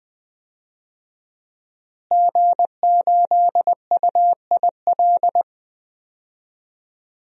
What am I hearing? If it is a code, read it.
Morse code, heard as a keyed tone, about 20 wpm, 700 Hz. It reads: G8UIL
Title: DXCC-100 Call Signs - 5 Characters - Repeated 3x - 20wpm